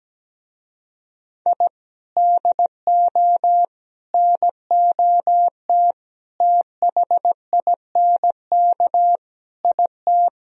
Morse code, heard as a keyed tone, about 17 wpm, 700 Hz. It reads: I DO NOT THINK IT
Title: Speed-Racing - Sets of 5 Words 17wpm